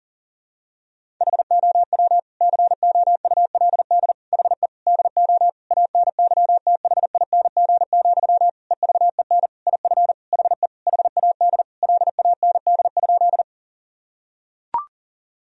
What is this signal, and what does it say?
Morse code, heard as a keyed tone, about 40 wpm, 700 Hz. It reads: HOW COULD HE DO ANYTHING, EVEN IF HE HAD LAND?
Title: Sentences from Top 1000 Words - Review Part 1 - 40wpm